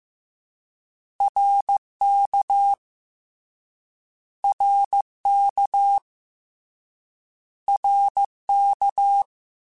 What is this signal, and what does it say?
Morse code, heard as a keyed tone, about 15 wpm, 800 Hz. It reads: RK RK RK
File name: Short Runs / Short Runs-mp3s / ShortContrasts @13